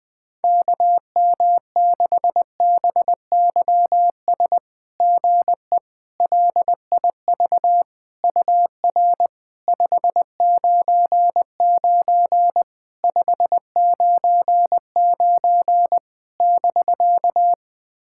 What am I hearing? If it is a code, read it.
Morse code, heard as a keyed tone, about 20 wpm, 690 Hz. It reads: KM6BYS GE LIV UR 599 599 <BK>
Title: SOTA QSO Practice 20wpm